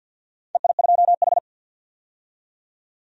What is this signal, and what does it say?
Morse code, heard as a keyed tone, about 50 wpm, 700 Hz. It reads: EI2H
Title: DXCC-100 Call Signs - 4 Characters - 50wpm